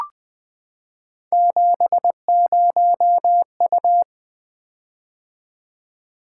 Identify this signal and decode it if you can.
Morse code, heard as a keyed tone, about 20 wpm, 700 Hz. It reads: 70U
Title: Three Letter-Number Combinations - 20wpm